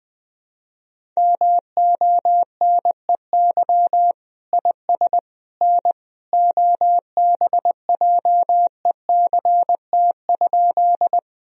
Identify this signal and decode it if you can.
Morse code, heard as a keyed tone, about 20 wpm, 700 Hz. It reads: MONEY IS N OBJECT?
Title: Sentences from Top 500 Words and Abbreviations 20wpm